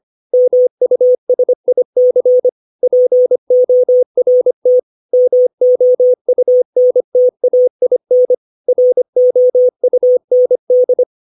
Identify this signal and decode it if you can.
Morse code, heard as a keyed tone, about 25 wpm, 500 Hz. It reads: MUSIC PORT MOUNTAIN ROUND
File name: 632 words / 632 Words 25WPM